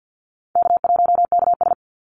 Morse code, heard as a keyed tone, about 50 wpm, 700 Hz. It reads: K1KS
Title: Easy Call Signs Repeated 3x - 50wpm